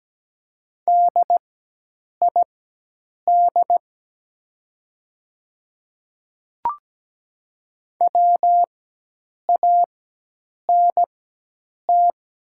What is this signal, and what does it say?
Morse code, heard as a keyed tone, about 17 wpm, spaced as for 8 wpm, 700 Hz. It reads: DID WANT
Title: 4x Character Spacing - Top 200 Words - 17wpm